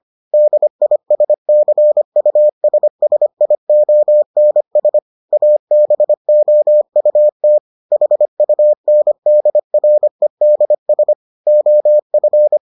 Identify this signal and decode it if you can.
Morse code, heard as a keyed tone, about 25 wpm, 600 Hz. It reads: DISCUSSIONS ABOUT HUNDREDS OF